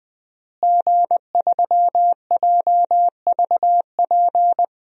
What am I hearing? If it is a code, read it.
Morse code, heard as a keyed tone, about 20 wpm, 700 Hz. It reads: G3JVP
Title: DXCC-100 Call Signs - 5 Characters - Repeated 2x - 20wpm